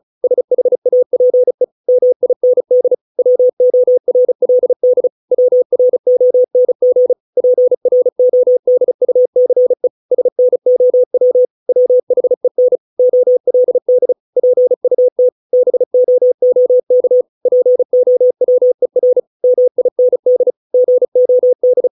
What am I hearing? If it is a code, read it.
Morse code, heard as a keyed tone, about 34 wpm, 500 Hz. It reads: SHAPE MIND WORLD WRONG PRODUCE SNOW WHEN OLD PUT BOOK POWER MIND GOD
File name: 632 words / 632 Words 35WPM